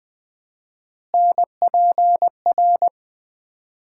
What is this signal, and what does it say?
Morse code, heard as a keyed tone, about 20 wpm, 700 Hz. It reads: NPR
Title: 4x Word-Spacing - Sets of 5 Words - 20wpm